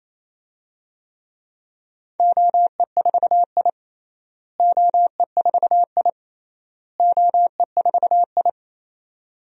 Morse code, heard as a keyed tone, about 27 wpm, 700 Hz. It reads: OE4I OE4I OE4I
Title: DXCC-100 Call Signs - 4 Characters - Repeated 3x - 28wpm